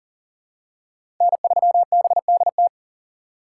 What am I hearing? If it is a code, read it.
Morse code, heard as a keyed tone, about 38 wpm, 700 Hz. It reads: N3BDT
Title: USA Call Signs - 5 Characters - 2x Character Spacing - 40wpm